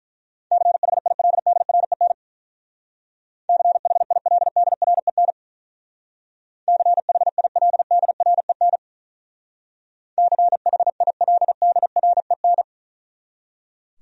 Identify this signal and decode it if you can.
Morse code, heard as a keyed tone, about 43 wpm, 700 Hz. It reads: CHILDREN CHILDREN CHILDREN CHILDREN
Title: Speed-Racing Top 300 Words 35wpm